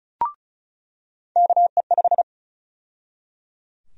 Morse code, heard as a keyed tone, about 34 wpm, 700 Hz. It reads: KE5
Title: USA Call Sign Prefixes - 35wpm